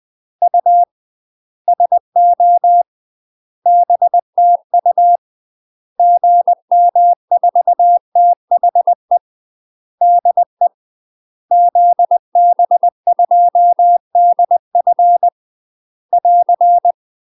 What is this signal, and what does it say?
Morse code, heard as a keyed tone, about 20 wpm, 700 Hz. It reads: U SO BTU GM4THE DE ZB2DF <AR>
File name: UK MASTER QSOs TRAINING/20WPM